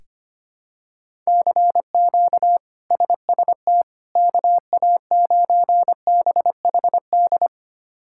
Morse code, heard as a keyed tone, about 25 wpm, 700 Hz. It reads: CQ SST KA9BHD
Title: K1USN SST QSO Practice - 25wpm